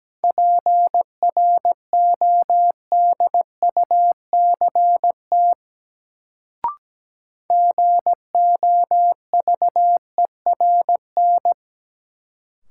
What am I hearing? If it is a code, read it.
Morse code, heard as a keyed tone, about 17 wpm, 700 Hz. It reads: PRODUCT GOVERN
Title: Sentences from Top 1000 Words - Review Part 2 - New Word Drill - 17wpm